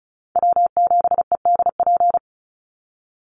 Morse code, heard as a keyed tone, about 34 wpm, 700 Hz. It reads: W7EDP
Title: USA Call Signs 35wpm